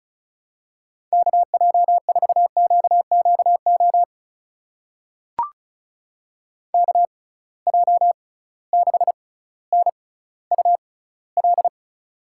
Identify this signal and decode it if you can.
Morse code, heard as a keyed tone, about 36 wpm, spaced as for 18 wpm, 700 Hz. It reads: KJ4QQO KJ6NUL
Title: USA Call Signs - 6 Characters - 6x Character Spacing - 35wpm